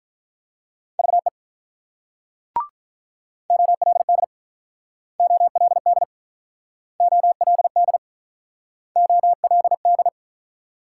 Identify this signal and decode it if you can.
Morse code, heard as a keyed tone, about 47 wpm, 700 Hz. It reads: VE OLD OLD OLD OLD